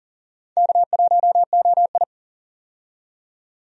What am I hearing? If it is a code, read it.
Morse code, heard as a keyed tone, about 40 wpm, 700 Hz. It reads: K1OI
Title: USA Call Signs - 4 Characters - 40wpm